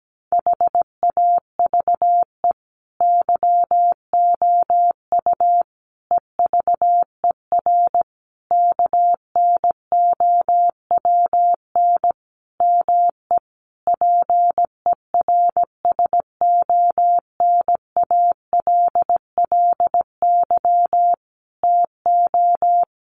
Morse code, heard as a keyed tone, about 17 wpm, 700 Hz. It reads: HAVE YOU EVER KNOWN ME PERSONALLY TO
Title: Sentences from Top 1400 Words 17wpm